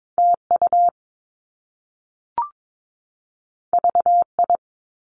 Morse code, heard as a keyed tone, about 22 wpm, 700 Hz. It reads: TU VI
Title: Top Two Letter Combinations in Words 22wpm